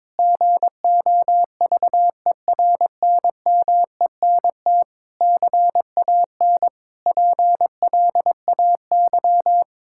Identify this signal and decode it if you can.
Morse code, heard as a keyed tone, about 21 wpm, 700 Hz. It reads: GOVERNMENT CAN PLAY
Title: Sentences from Top 1000 Words - Review Part 5 - 22wpm